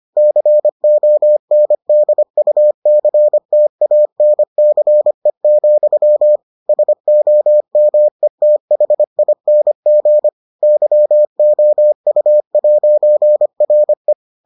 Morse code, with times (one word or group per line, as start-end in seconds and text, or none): 0.2-6.4 CONDUCTANCE,
6.7-10.3 SOMETHING
10.6-14.1 YOU'RE